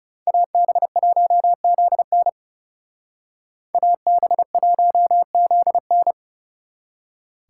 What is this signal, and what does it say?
Morse code, heard as a keyed tone, about 36 wpm, 700 Hz. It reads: AB1ZN AB1ZN
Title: Speed Racing - USA 2x2 Call Signs 30wpm